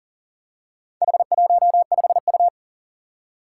Morse code, heard as a keyed tone, about 43 wpm, 700 Hz. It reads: H15U
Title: Four Challenging Letter-Number Combinations 40wpm